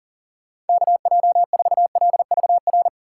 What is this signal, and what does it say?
Morse code, heard as a keyed tone, about 40 wpm, 700 Hz. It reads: KJ4LVR
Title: USA Call Signs - 6 Characters - 4x Character Spacing - 40wpm